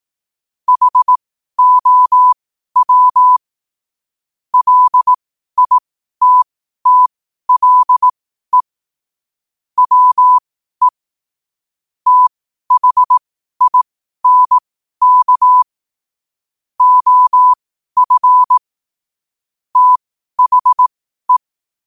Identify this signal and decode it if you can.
Morse code, heard as a keyed tone, about 18 wpm, spaced as for 12 wpm, 1000 Hz. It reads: HOW LITTLE WE THINK OF THE